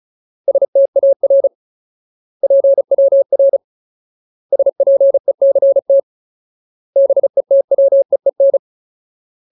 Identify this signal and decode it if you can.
Morse code, heard as a keyed tone, about 34 wpm, 550 Hz. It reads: STAR PWR SPECT BETWEEN
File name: Ultimate Word List / 35WPM Ultimate